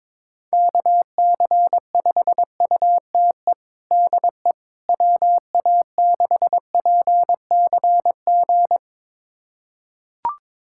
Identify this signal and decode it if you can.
Morse code, heard as a keyed tone, about 22 wpm, 700 Hz. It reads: KC5UTE DE WA6PCG
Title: USA Callsign DE Callsign 22wpm